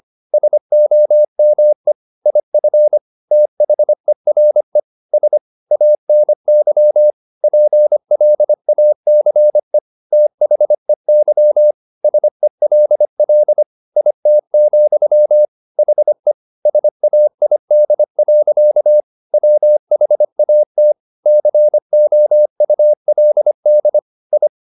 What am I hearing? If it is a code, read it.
Morse code, heard as a keyed tone, about 25 wpm, 600 Hz. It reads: SOME IF THERE S ANY PLACE THEY SELL IT, HE SAID. WHAT COULD I